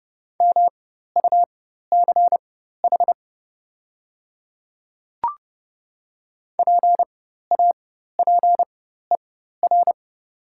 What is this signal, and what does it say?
Morse code, heard as a keyed tone, about 30 wpm, spaced as for 14 wpm, 700 Hz. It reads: MUCH PAPER
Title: 4x Character Spacing - Top 300 Words - 30wpm